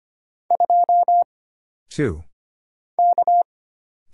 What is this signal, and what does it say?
Morse code, heard as a keyed tone, about 25 wpm, 700 Hz. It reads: 2 K
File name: Single Character (Letter:Number:Punctuation:Prosign) - Mind-Melt - 25wpm